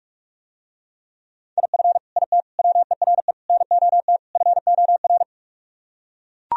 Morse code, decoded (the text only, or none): IF IT WERE NOT FOR